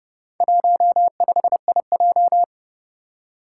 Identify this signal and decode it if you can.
Morse code, heard as a keyed tone, about 30 wpm, 700 Hz. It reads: 15IJ
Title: Four Challenging Letter-Number Combinations 30wpm